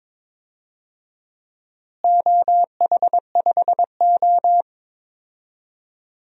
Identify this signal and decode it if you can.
Morse code, heard as a keyed tone, about 22 wpm, 700 Hz. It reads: OH5O
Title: DXCC-100 Call Signs - 4 Characters - Repeated 2x - 22wpm